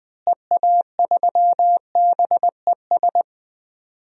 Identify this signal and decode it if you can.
Morse code, heard as a keyed tone, about 20 wpm, 700 Hz. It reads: EA3BES
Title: DXCC-100 Call Signs - 6 Characters - Repeated 3x - 20wpm